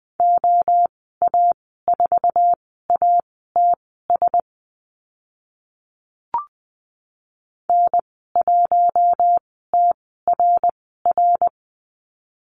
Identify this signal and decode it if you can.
Morse code, heard as a keyed tone, about 20 wpm, spaced as for 14 wpm, 700 Hz. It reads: OA4ATS N1TRR
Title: 062 Lesson 11g - Letter h - Call Signs 20wpm